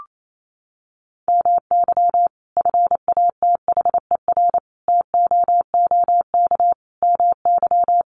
Morse code, heard as a keyed tone, about 28 wpm, 700 Hz. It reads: MY FATHER TOOK MY
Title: Sentences from Top 1500 Words 28wpm